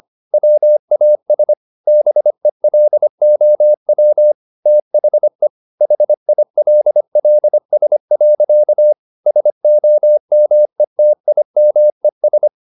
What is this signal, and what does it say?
Morse code, heard as a keyed tone, about 25 wpm, 600 Hz. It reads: WAS BELOW THE HILLS. SOMETIMES
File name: the old man and the sea part 1 25wpm